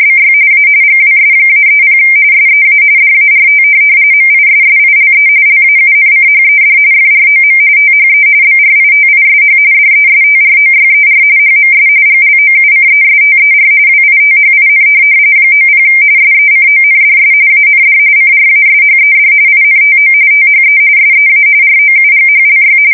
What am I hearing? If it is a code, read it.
RTTY, about 45.45 baud. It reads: BY A GWS-30 SEA DART MISSILE LAUNCHED FROM THE BRITISH DESTROYER HMS GLOUCESTER WITHIN 90 SECONDS AND CRASHED INTO THE SEA ROUGHLY
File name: fsk170 / USS-Missouri